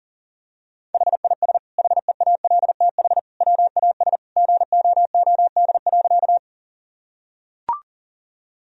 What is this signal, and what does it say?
Morse code, heard as a keyed tone, about 40 wpm, 700 Hz. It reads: HIS HEALTH WAS GOOD.
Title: Sentences from Top 1000 Words - Review Part 3 - 40wpm